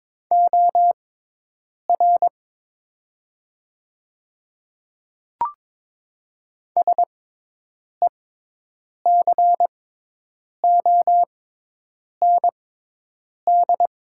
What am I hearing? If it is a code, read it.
Morse code, heard as a keyed tone, about 22 wpm, spaced as for 8 wpm, 700 Hz. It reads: OR SECOND
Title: 6x Character Spacing - Top 300 Words - 22wpm